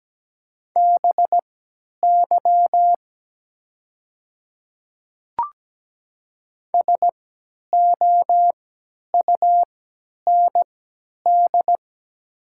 Morse code, heard as a keyed tone, about 17 wpm, spaced as for 10 wpm, 700 Hz. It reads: BY SOUND
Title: 3x Character Spacing - Top 100 Words - 17wpm